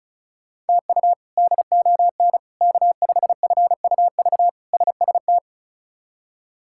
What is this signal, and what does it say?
Morse code, heard as a keyed tone, about 34 wpm, 700 Hz. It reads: TU DON K5FUV SST
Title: K1USN SST QSO Practice - 35wpm